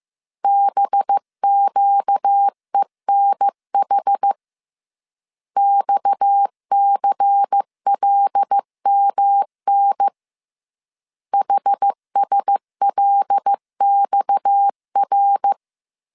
Morse code, heard as a keyed tone, about 15 wpm, 800 Hz. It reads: BQENH XCLMN HSLXR
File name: CW / DET / 1 6 1500C S C 14wpm 2951